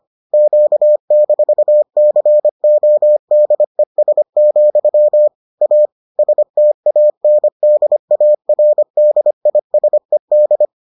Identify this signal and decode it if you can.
Morse code, heard as a keyed tone, about 25 wpm, 600 Hz. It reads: Q-CODES, A STANDARDISED